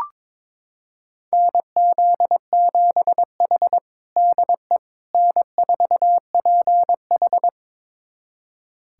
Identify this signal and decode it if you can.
Morse code, heard as a keyed tone, about 22 wpm, 700 Hz. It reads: NZ7H DE N4PH
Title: USA Callsign DE Callsign - 4 Characters - 22wpm